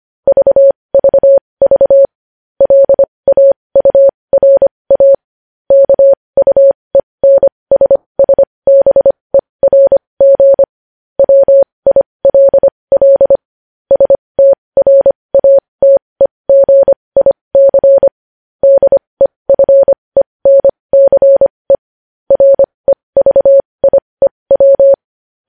Morse code, heard as a keyed tone, about 25 wpm, spaced as for 20 wpm, 550 Hz. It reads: VVV LAURA KUENSSBERG WILL STRATEGIC DEFENCE REVIEW
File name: podcast / News headlines-1x-25x20